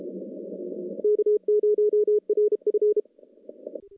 Morse code, heard as a keyed tone, about 32 wpm, 420 Hz. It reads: K0RF
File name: contests / CW-SS-k5tr- -1300